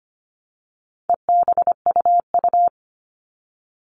Morse code, heard as a keyed tone, about 25 wpm, 700 Hz. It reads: EBUU
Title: All Cut Numbers - 4-Digit 25wpm